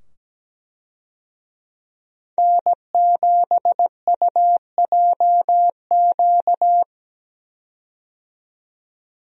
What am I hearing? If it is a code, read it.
Morse code, heard as a keyed tone, about 17 wpm, 700 Hz. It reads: N7UJQ
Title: USA Call Signs - 5 Characters - 17wpm